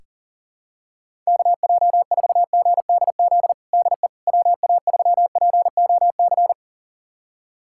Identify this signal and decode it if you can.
Morse code, heard as a keyed tone, about 40 wpm, 700 Hz. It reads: KJ4GDZ DE WA3POC
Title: USA Callsign DE Callsign 40wpm